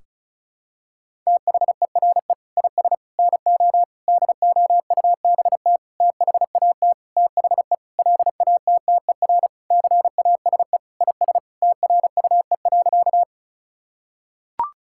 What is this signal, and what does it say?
Morse code, heard as a keyed tone, about 34 wpm, 700 Hz. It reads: THERE IS NO DOUBT THAT THE LATTER CASE IS TRUE.
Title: Sentences from Top 1600 Words 35wpm